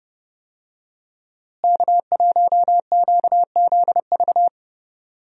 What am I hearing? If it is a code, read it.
Morse code, heard as a keyed tone, about 31 wpm, 700 Hz. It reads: K1QZV
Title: USA Call Signs - 5 Characters - Repeated 2x - 30wpm